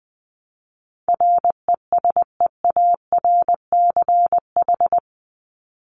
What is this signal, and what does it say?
Morse code, heard as a keyed tone, about 20 wpm, 700 Hz. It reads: RESEARCH
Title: Top 501-750 Words 20wpm